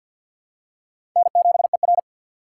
Morse code, heard as a keyed tone, about 53 wpm, 700 Hz. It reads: N7ER
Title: USA Call Signs - 4 Characters - 6x Character Spacing - 50wpm